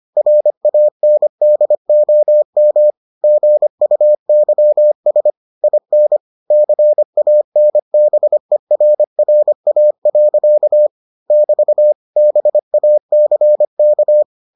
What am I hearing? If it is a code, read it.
Morse code, heard as a keyed tone, about 25 wpm, 600 Hz. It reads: RANDOM GUYS IN CANBERRA. <BT> BACK